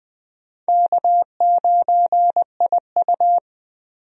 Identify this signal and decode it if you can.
Morse code, heard as a keyed tone, about 20 wpm, 700 Hz. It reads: K9IU
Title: USA Call Signs - 4 Characters - 20wpm